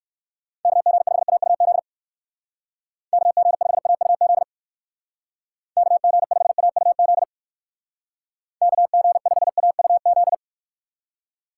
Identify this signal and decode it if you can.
Morse code, heard as a keyed tone, about 56 wpm, 700 Hz. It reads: KG5AUZ KG5AUZ KG5AUZ KG5AUZ
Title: Speed Racing - USA 2x3 Call Signs 45wpm